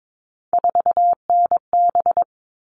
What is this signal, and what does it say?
Morse code, heard as a keyed tone, about 22 wpm, 700 Hz. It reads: 4NB
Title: All Cut Numbers - 3-Digit 22wpm